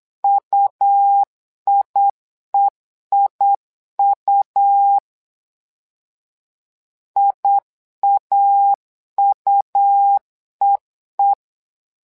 Morse code, heard as a keyed tone, about 8 wpm, 800 Hz. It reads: UIEIU IAUEE
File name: LEZIONI / 2 UHDN / 2 UH 8wpm 2129